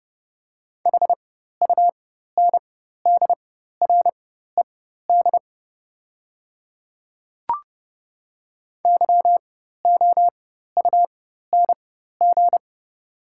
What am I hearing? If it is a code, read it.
Morse code, heard as a keyed tone, about 30 wpm, spaced as for 14 wpm, 700 Hz. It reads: HUNDRED YOUNG
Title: 4x Character Spacing - Top 500 Words - 30wpm